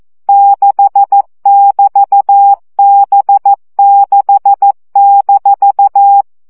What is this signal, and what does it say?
Morse code, heard as a keyed tone, about 14 wpm, 800 Hz. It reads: 6<BT>B6-
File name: CW gyakorlas b 6 egyenlo minusz 12 baud